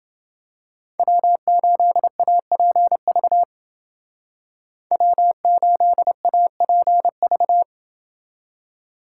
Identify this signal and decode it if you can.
Morse code, heard as a keyed tone, about 30 wpm, 700 Hz. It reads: W8APV W8APV